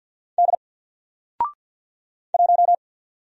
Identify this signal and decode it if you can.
Morse code, heard as a keyed tone, about 52 wpm, 700 Hz. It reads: D 1
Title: Single Letter-Number Alt 50wpm